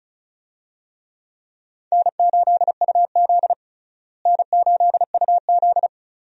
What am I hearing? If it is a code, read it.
Morse code, heard as a keyed tone, about 36 wpm, 700 Hz. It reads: N8UZ N8UZ
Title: USA Call Signs - 4 Characters - Repeated 3x - 35wpm